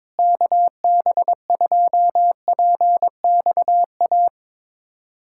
Morse code, heard as a keyed tone, about 22 wpm, 700 Hz. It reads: KB2PXA
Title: USA Callsign DE Callsign - 6 Characters - 22wpm